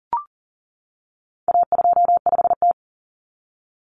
Morse code, heard as a keyed tone, about 40 wpm, 700 Hz. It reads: A25T
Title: Common Cut Numbers - 4-Digit 40wpm